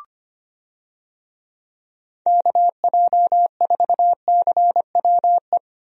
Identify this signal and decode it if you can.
Morse code, heard as a keyed tone, about 25 wpm, 700 Hz. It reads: KJ4CWE
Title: USA Call Signs - 6 Characters - Repeated 2x - 25wpm